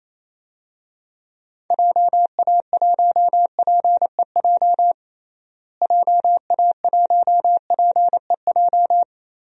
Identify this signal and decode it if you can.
Morse code, heard as a keyed tone, about 28 wpm, 700 Hz. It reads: JA1PEJ JA1PEJ
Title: DXCC-100 Call Signs - 6 Characters - Repeated 3x - 28wpm